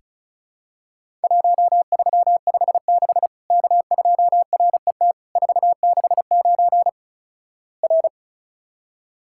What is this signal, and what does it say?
Morse code, heard as a keyed tone, about 36 wpm, 700 Hz. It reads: 1356 K2RET 469 R
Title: Worked All Europe DX Contest - QTC Exchange - 35wpm